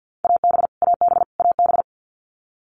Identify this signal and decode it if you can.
Morse code, heard as a keyed tone, about 50 wpm, 700 Hz. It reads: AB AB AB
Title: Canadian Provinces and Territories - Repeated 3x - 50wpm